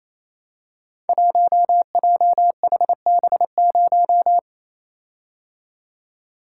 Morse code, heard as a keyed tone, about 28 wpm, 700 Hz. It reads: 1JHB0
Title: Code Groups - 5 Letter:Numbers - 28wpm